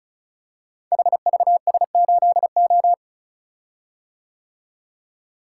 Morse code, heard as a keyed tone, about 36 wpm, 700 Hz. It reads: HVS8O
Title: Code Groups - 5 Letter:Numbers - 35wpm